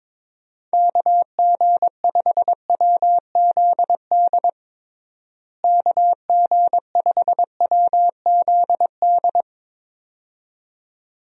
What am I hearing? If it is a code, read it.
Morse code, heard as a keyed tone, about 22 wpm, 700 Hz. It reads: KG5WZD KG5WZD
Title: USA Call Signs - 6 Characters - Repeated 2x - 22wpm